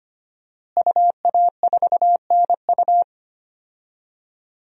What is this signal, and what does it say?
Morse code, heard as a keyed tone, about 25 wpm, 700 Hz. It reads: UA4NU
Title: DXCC-100 Call Signs - 5 Characters - Repeated 3x - 25wpm